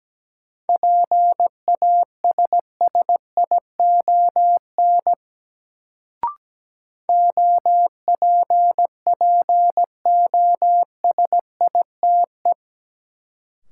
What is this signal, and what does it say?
Morse code, heard as a keyed tone, about 17 wpm, 700 Hz. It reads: PASSION OPPOSITE
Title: Sentences from Top 1300 Words - New Word Drill - 17wpm